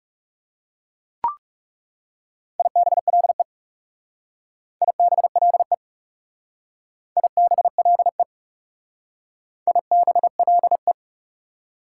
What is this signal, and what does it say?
Morse code, heard as a keyed tone, about 38 wpm, 690 Hz. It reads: IBLE IBLE IBLE IBLE